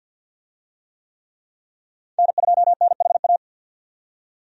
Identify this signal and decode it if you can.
Morse code, heard as a keyed tone, about 53 wpm, 700 Hz. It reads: N2NHA
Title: USA Call Signs - 5 Characters - 50wpm